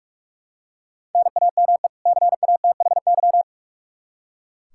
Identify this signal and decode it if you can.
Morse code, heard as a keyed tone, about 46 wpm, 700 Hz. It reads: NAME CATHY
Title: Practice QSO - Exchange 1 - Beginning - 45wpm